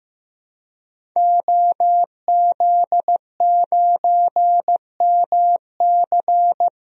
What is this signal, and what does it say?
Morse code, heard as a keyed tone, about 16 wpm, 700 Hz. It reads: OZ9MC
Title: DXCC-100 Call Signs - 5 Characters - 15wpm